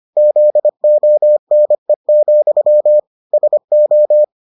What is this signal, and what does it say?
Morse code, heard as a keyed tone, about 25 wpm, 600 Hz. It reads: ZONE, SO